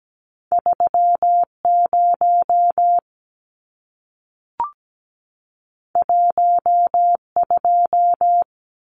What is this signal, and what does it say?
Morse code, heard as a keyed tone, about 17 wpm, 700 Hz. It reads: 30 12
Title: Numbers - 2-Digit 17wpm